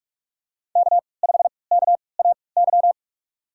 Morse code, heard as a keyed tone, about 44 wpm, spaced as for 25 wpm, 700 Hz. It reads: K5KAY
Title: USA Call Signs - 5 Characters - 3x Character Spacing - 45wpm